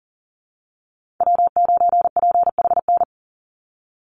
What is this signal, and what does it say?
Morse code, heard as a keyed tone, about 38 wpm, 700 Hz. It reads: W9PHN
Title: USA Call Signs 40wpm